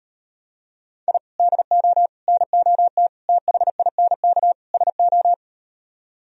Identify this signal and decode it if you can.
Morse code, heard as a keyed tone, about 38 wpm, 700 Hz. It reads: I DO NOT THINK SO